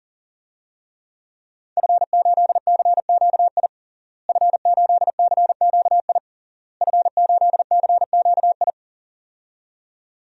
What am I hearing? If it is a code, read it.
Morse code, heard as a keyed tone, about 40 wpm, 700 Hz. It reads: F8CQI F8CQI F8CQI
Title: DXCC-100 Call Signs - 5 Characters - Repeated 3x - 40wpm